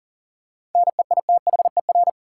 Morse code, heard as a keyed tone, about 40 wpm, 700 Hz. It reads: NEITHER
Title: Seven Letter Words - 40wpm